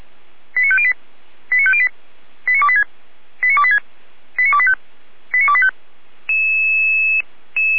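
Five-tone selective call (ZVEI), heard as quick runs of five tones, tones of about 70 ms. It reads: 89508 89508 89286 89286 89285 89285
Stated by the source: ZVEI five-tone call received by SDR, from audio.f85360.feuerwehr-orlamuende.org, fire brigade Orlamünde.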